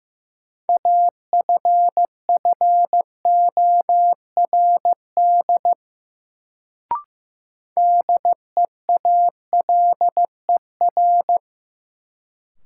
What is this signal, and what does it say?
Morse code, heard as a keyed tone, about 15 wpm, 700 Hz. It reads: AFFORD DEALER
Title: Sentences from Top 1100 Words - New Word Drill Alt - 15wpm